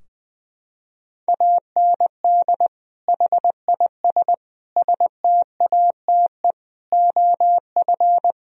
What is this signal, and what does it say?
Morse code, heard as a keyed tone, about 20 wpm, 700 Hz. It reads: AND HIS STATE OF
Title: Sentences from Top 1000 Words - Review Part 2 - 20wpm